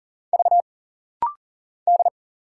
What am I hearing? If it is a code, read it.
Morse code, heard as a keyed tone, about 40 wpm, 700 Hz. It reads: V D
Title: Single Letters - Rapid-Fire 40wpm